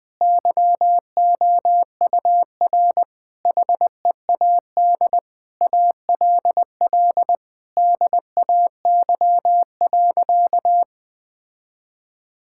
Morse code, heard as a keyed tone, about 20 wpm, 700 Hz. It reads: YOUR HEAD ALL DAY.
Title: Difficult Sentences with Hints - 20wpm